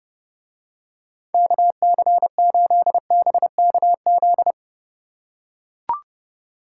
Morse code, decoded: KC8BKZ

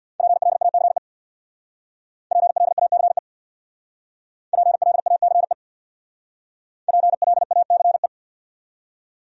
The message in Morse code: PLACE PLACE PLACE PLACE

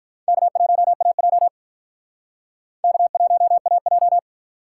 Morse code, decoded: K1AJ K1AJ

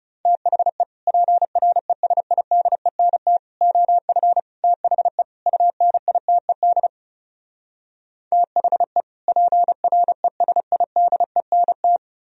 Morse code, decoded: THE PRESIDENT OF THE UNITED THE PRESIDENT

